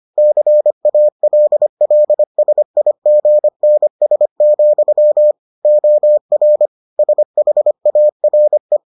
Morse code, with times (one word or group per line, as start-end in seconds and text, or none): 0.2-5.3 CALLSIGNS,
5.7-6.7 OR
7.0-8.8 SHARE